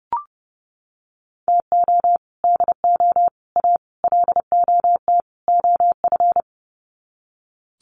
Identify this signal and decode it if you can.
Morse code, heard as a keyed tone, about 30 wpm, 700 Hz. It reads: TO DO A LOT OF
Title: Sets of 5 Words Encore 30wpm